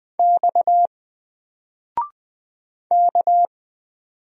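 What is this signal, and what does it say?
Morse code, heard as a keyed tone, about 20 wpm, 700 Hz. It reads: X K
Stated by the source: Single letter and number Morse drill at 20 wpm, from N4LCD.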